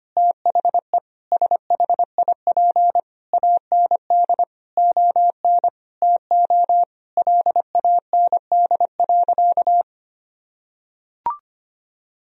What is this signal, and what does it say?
Morse code, heard as a keyed tone, about 25 wpm, 700 Hz. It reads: THE SHIP AND ON TO LAND.
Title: Sentences from Top 1000 Words - Review Part 2 - 25wpm